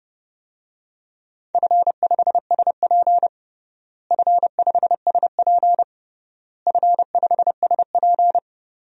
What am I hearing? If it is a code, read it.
Morse code, heard as a keyed tone, about 30 wpm, 700 Hz. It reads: F5SP F5SP F5SP
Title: DXCC-100 Call Signs - 4 Characters - Repeated 3x - 30wpm